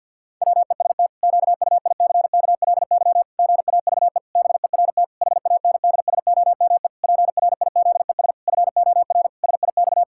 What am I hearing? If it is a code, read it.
Morse code, heard as a keyed tone, about 50 wpm, 700 Hz. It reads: WEST QUICKLY GAVE BERT HANDSOME PRIZES FOR SIX
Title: Pangram Challenge v2 - 50wpm